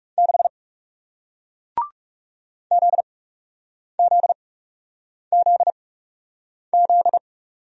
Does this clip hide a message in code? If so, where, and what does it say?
Yes: Morse code from 0.2 to 7.2 s, heard as a keyed tone, about 40 wpm, spaced as for 8 wpm, 700 Hz. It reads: 6 ZZZZ